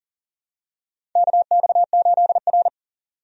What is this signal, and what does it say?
Morse code, heard as a keyed tone, about 40 wpm, 700 Hz. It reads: KX8R
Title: USA Call Signs - 4 Characters - 40wpm